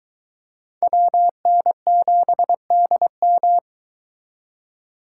Morse code, heard as a keyed tone, about 23 wpm, 700 Hz. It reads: WN7DM